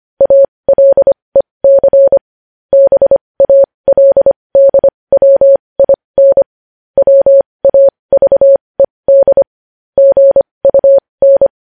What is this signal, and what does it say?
Morse code, heard as a keyed tone, about 25 wpm, spaced as for 20 wpm, 550 Hz. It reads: ALEC BALDWIN WAVED GUN
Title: News Headlines at 20